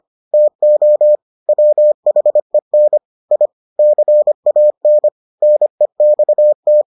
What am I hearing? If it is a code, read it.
Morse code, heard as a keyed tone, about 25 wpm, 600 Hz. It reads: TO WHEN I CAN NEXT